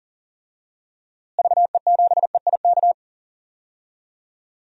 Morse code, heard as a keyed tone, about 37 wpm, 700 Hz. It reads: VE7EIK
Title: DXCC-100 Call Signs - 6 Characters - Repeated 3x - 40wpm